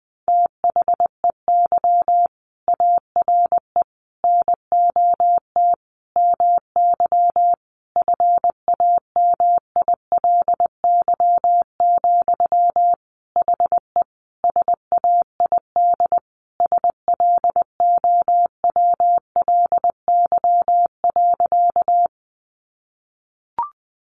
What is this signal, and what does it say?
Morse code, heard as a keyed tone, about 20 wpm, 700 Hz. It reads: THEY ARE NOT MY FAMILY, HE SAID SLOWLY.
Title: Sentences from Top 1100 Words 20wpm